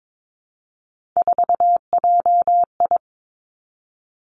Morse code, heard as a keyed tone, about 22 wpm, 700 Hz. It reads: 4JI
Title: Three Challenging Letter-Number Combinations 22wpm